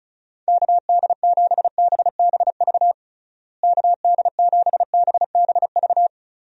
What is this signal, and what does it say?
Morse code, heard as a keyed tone, about 36 wpm, 700 Hz. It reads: KD7BBV KD7BBV
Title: USA Call Signs - 6 Characters - Repeated 2x - 35wpm